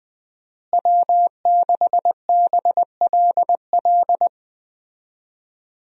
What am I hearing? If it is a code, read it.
Morse code, heard as a keyed tone, about 21 wpm, 700 Hz. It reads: W6BLL
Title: USA Call Signs - 5 Characters - Repeated 2x - 20wpm